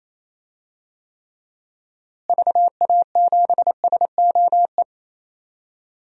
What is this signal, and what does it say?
Morse code, heard as a keyed tone, about 27 wpm, 700 Hz. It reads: VA7SOE
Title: DXCC-100 Call Signs - 6 Characters - Repeated 2x - 28wpm